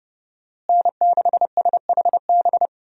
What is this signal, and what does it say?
Morse code, heard as a keyed tone, about 30 wpm, 700 Hz. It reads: N6SHB
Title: USA Call Signs - 5 Characters - Repeated 2x - 30wpm